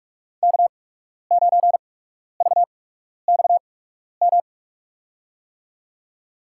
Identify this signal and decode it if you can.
Morse code, heard as a keyed tone, about 46 wpm, spaced as for 12 wpm, 700 Hz. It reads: K9VXM
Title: USA Call Signs - 8x Character Spacing - 45wpm